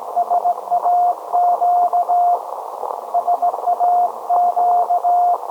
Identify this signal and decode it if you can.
Morse code, heard as a keyed tone, about 17 wpm, 710 Hz. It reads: SAQ SAQ
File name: storage / message Grimeton SAQ